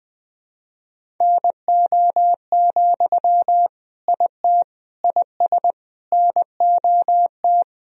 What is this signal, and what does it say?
Morse code, heard as a keyed tone, about 21 wpm, 700 Hz. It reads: NO, IT IS NOT
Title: Sentences from Top 1000 Words - Review Part 1 - 20wpm